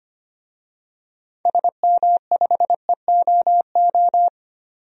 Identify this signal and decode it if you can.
Morse code, heard as a keyed tone, about 25 wpm, 700 Hz. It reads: SM5EOO
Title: DXCC-100 Call Signs - 6 Characters - 25wpm